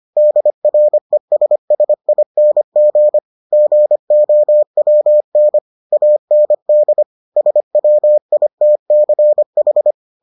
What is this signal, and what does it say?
Morse code, heard as a keyed tone, about 25 wpm, 600 Hz. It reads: DRESSING GOWN AND SWITCH